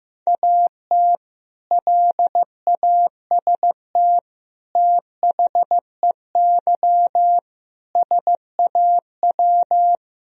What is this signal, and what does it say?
Morse code, heard as a keyed tone, about 16 wpm, 700 Hz. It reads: AT LAST THEY SAW
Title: Sentences from Top 1000 Words - Review Part 1 - 15wpm